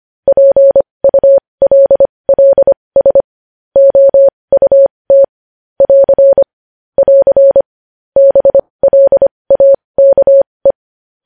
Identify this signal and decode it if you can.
Morse code, heard as a keyed tone, about 25 wpm, spaced as for 20 wpm, 550 Hz. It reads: PULLS OUT <AR> <AR> BLAKE
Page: news-headlines-in-morse-code-at-20-wpm